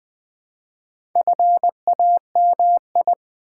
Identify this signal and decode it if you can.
Morse code, heard as a keyed tone, about 20 wpm, 700 Hz. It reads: FAMI
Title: Build A Word - 2 Letters At A Time - 6-Letter Words - 20wpm